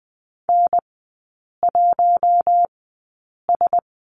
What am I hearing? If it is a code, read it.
Morse code, heard as a keyed tone, about 20 wpm, spaced as for 8 wpm, 700 Hz. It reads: N1S
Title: 041 Lesson 8e - Number 1 - Three Characters 20wpm